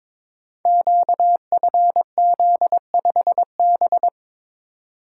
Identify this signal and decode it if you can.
Morse code, heard as a keyed tone, about 22 wpm, 700 Hz. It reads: QFZ5B
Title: Code Groups - 5 Letter:Numbers - 22wpm